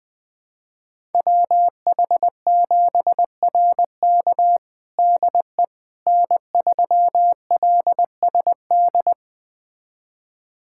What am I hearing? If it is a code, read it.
Morse code, heard as a keyed tone, about 20 wpm, 700 Hz. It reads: WH7RK DE N3LSD
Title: USA Callsign DE Callsign - 5 Characters - 20wpm